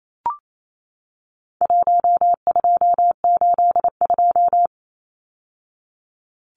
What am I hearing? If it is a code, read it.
Morse code, heard as a keyed tone, about 28 wpm, 700 Hz. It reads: 1282
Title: Numbers - 4-Digit 28wpm